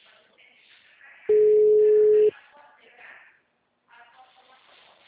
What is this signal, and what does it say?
Ringback tone (Europe, 425 Hz), with speech.